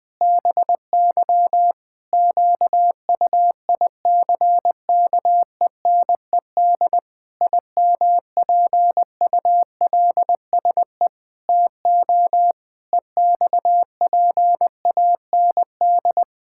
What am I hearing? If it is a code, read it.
Morse code, heard as a keyed tone, about 20 wpm, 700 Hz. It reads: BY QUICKENED IMPULSE TO EXPAND
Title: Pangram Challenge v2 - 20wpm